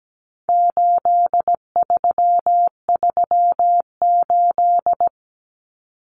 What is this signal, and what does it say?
Morse code, heard as a keyed tone, about 17 wpm, 700 Hz. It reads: 8338
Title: Numbers - 4-Digit 17wpm